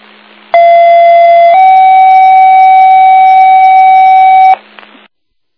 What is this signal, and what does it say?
Two-tone page, heard as a one-second tone, then a three-second tone, about 680 then 760 Hz, with noise. It is heard from 0.5 to 4.5 s.